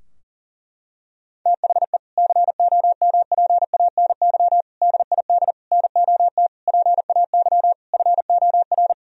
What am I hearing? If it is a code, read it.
Morse code, heard as a keyed tone, about 40 wpm, 700 Hz. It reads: THE COMPANY DID NOT PAY FOR
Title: Sentences from Top 1000 Words - Review Part 3 - 40wpm